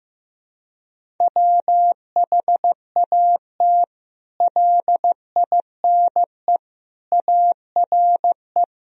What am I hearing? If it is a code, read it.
Morse code, heard as a keyed tone, about 15 wpm, 700 Hz. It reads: WHAT LINE ARE
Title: Sentences from Top 1000 Words - Review Part 1 - 15wpm